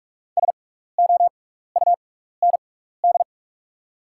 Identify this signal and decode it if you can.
Morse code, heard as a keyed tone, about 44 wpm, spaced as for 15 wpm, 690 Hz. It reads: SOUND
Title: 6x Character Spacing - Top 200 Words - 45wpm